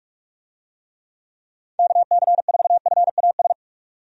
Morse code, heard as a keyed tone, about 46 wpm, 700 Hz. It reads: KC4FAS
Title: USA Call Signs - 6 Characters - 45wpm